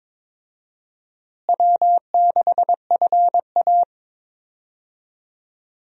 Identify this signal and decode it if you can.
Morse code, heard as a keyed tone, about 22 wpm, 700 Hz. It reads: W6FA
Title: USA Call Signs - 4 Characters - Repeated 3x - 22wpm